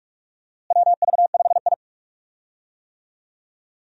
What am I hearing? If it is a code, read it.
Morse code, heard as a keyed tone, about 46 wpm, 700 Hz. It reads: WV5I
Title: USA Call Signs - 4 Characters - Repeated 2x - 45wpm